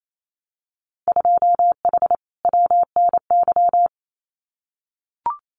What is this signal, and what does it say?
Morse code, heard as a keyed tone, about 28 wpm, 700 Hz. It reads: 2H WNY
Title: Winter Field Day Exchanges Encore 28wpm